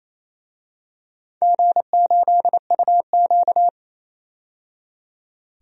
Morse code, heard as a keyed tone, about 28 wpm, 700 Hz. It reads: G8UQ
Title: DXCC-100 Call Signs - 4 Characters - 28wpm